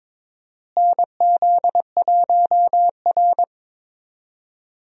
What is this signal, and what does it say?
Morse code, heard as a keyed tone, about 21 wpm, 700 Hz. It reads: NZ1R
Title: USA Call Signs - 4 Characters - Repeated 3x - 22wpm